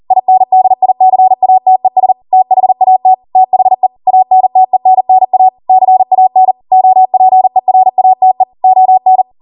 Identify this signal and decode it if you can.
Morse code, heard as a keyed tone, about 40 wpm, 750 Hz. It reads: INDICATES THAT THE ANTENNA CAN OPERATE ON